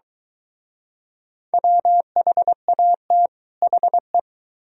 Morse code, heard as a keyed tone, about 23 wpm, 700 Hz. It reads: WHAT HE